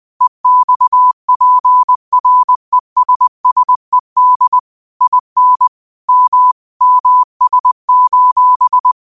Morse code, heard as a keyed tone, about 20 wpm, 1000 Hz. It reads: EXPRESSED IN M MS:
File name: qotdfiles / 20wpm